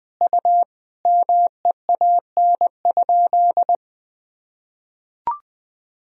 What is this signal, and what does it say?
Morse code, heard as a keyed tone, about 20 wpm, 700 Hz. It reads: U MEAN?
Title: Sentences from Top 200 Words and Abbreviations 20wpm